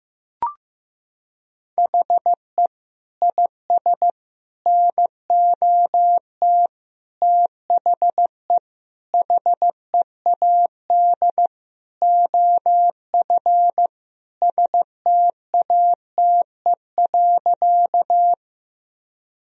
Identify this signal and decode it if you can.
Morse code, heard as a keyed tone, about 15 wpm, 700 Hz. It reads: HE IS NOT THE HEAD OF STATE.
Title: Sentences from Top 1000 Words - Review Part 2 - 15wpm